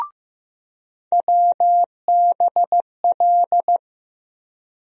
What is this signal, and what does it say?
Morse code, heard as a keyed tone, about 15 wpm, 700 Hz. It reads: WBL
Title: USA Call Sign Postfixes - 3 Letters 15wpm